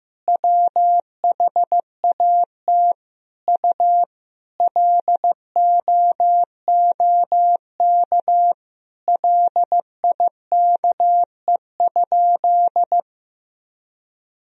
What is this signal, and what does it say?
Morse code, heard as a keyed tone, about 15 wpm, 700 Hz. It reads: WHAT U LOOK LIKE?
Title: Sentences from Top 100 Words and Abbreviations 15wpm